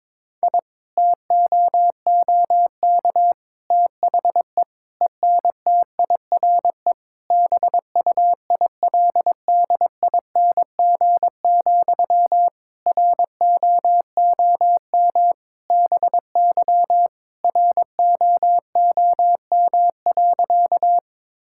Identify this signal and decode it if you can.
Morse code, heard as a keyed tone, about 21 wpm, 700 Hz. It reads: I TOOK THE ENTIRE BUILDING, ROOM BY ROOM.
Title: Sentences from Top 1000 Words - Review Part 5 - 22wpm